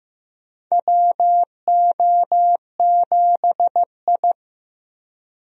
Morse code, heard as a keyed tone, about 15 wpm, 700 Hz. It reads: WO7I
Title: USA Call Signs - 4 Characters - Repeated 3x - 15wpm